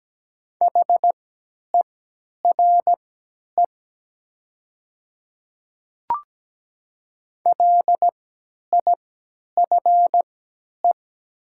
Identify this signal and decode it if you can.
Morse code, heard as a keyed tone, about 17 wpm, spaced as for 10 wpm, 700 Hz. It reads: HERE LIFE
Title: 3x Character Spacing - Top 300 Words - 17wpm